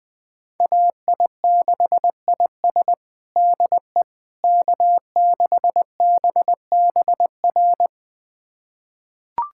Morse code, heard as a keyed tone, about 20 wpm, 700 Hz. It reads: AI6IS DE K6BBR
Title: USA Callsign DE Callsign - 5 Characters - 20wpm